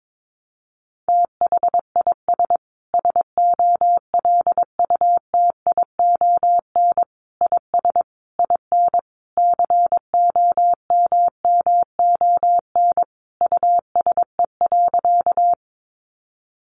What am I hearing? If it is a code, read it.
Morse code, heard as a keyed tone, about 22 wpm, 700 Hz. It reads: THIS SOLUTION IS IN COMMON USE.
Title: Sentences from Top 1300 Words 22wpm